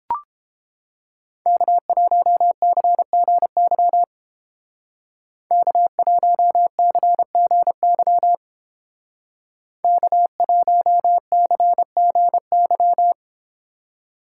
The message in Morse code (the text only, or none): K1CGY K1CGY K1CGY